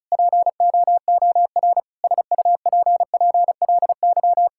Morse code, heard as a keyed tone, about 34 wpm, 700 Hz. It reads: POOR SUPPLY
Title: Sentences from Top 1000 Words - Review Part 5 - 35wpm